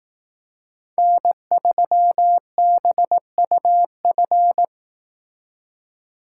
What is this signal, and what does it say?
Morse code, heard as a keyed tone, about 18 wpm, 700 Hz. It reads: N3BUF
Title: Speed Racing - USA Call Signs 15wpm